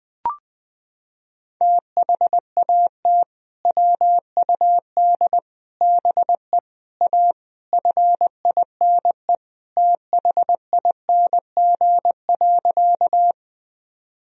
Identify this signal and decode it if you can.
Morse code, heard as a keyed tone, about 20 wpm, 700 Hz. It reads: THAT WUD BE A FINE THING.
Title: Sentences from Top 500 Words and Abbreviations 20wpm